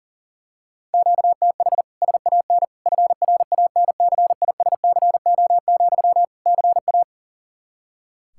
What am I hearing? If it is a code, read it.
Morse code, heard as a keyed tone, about 40 wpm, 700 Hz. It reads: QTH SAN FRANCISCO, CA
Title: Practice QSO - Exchange 1 - Beginning - 40wpm